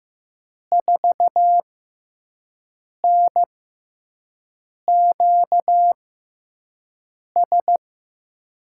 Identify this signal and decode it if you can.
Morse code, heard as a keyed tone, about 15 wpm, spaced as for 5 wpm, 700 Hz. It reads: 4NQS
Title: USA Call Signs - 6 Characters - 6x Character Spacing - 15wpm